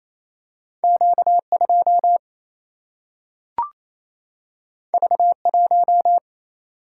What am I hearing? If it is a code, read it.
Morse code, heard as a keyed tone, about 28 wpm, 700 Hz. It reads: Q 2 V 1
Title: Single-Letter Single-Number 28wpm